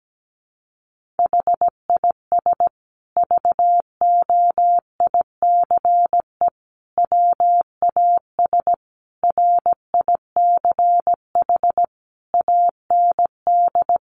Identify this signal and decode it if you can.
Morse code, heard as a keyed tone, about 17 wpm, 700 Hz. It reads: HIS VOICE WAS RICH AND
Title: Sentences from Top 1100 Words 17wpm